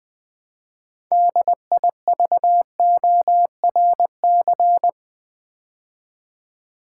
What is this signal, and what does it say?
Morse code, heard as a keyed tone, about 20 wpm, 700 Hz. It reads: DIVORC
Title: Build A Word - 2 Letters At A Time - 8-Letter Words - 20wpm